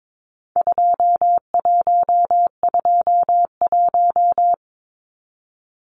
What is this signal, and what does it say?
Morse code, heard as a keyed tone, about 22 wpm, 700 Hz. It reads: 2121
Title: Numbers - 4-Digit 22wpm